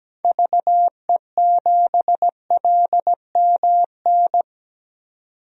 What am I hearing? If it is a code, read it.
Morse code, heard as a keyed tone, about 17 wpm, 700 Hz. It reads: VE7LMN
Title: DXCC-100 Call Signs - 6 Characters - 17wpm